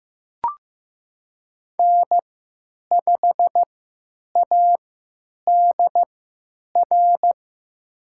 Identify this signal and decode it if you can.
Morse code, heard as a keyed tone, about 15 wpm, spaced as for 8 wpm, 700 Hz. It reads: N5ADR
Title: USA Call Signs - 5 Characters - 3x Character Spacing - 15wpm